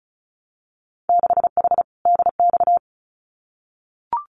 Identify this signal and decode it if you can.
Morse code, heard as a keyed tone, about 34 wpm, 700 Hz. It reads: 6H DX
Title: Winter Field Day Exchanges Encore 35wpm